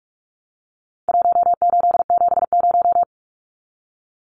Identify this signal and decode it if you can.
Morse code, heard as a keyed tone, about 44 wpm, 700 Hz. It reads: 1870
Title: Numbers - Contest Style - 4-Digit 45wpm